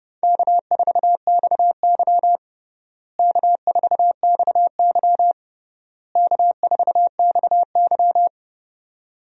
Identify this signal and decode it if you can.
Morse code, heard as a keyed tone, about 30 wpm, 700 Hz. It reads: K4XY K4XY K4XY
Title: USA Call Signs - 4 Characters - Repeated 3x - 30wpm